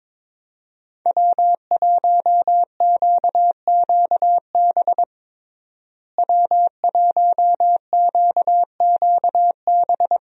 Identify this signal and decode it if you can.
Morse code, heard as a keyed tone, about 22 wpm, 700 Hz. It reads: W1QQB W1QQB
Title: USA Call Signs - 5 Characters - Repeated 2x - 22wpm